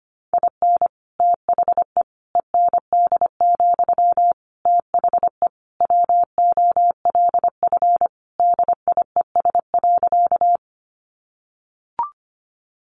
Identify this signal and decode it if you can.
Morse code, heard as a keyed tone, about 25 wpm, 700 Hz. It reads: IN THE END, THE WOLF DIES.
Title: Additional Easy Phrases 25wpm